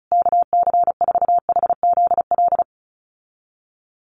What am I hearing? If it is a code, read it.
Morse code, heard as a keyed tone, about 34 wpm, 700 Hz. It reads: KC4HZL
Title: USA Call Signs Repeated 3x 35wpm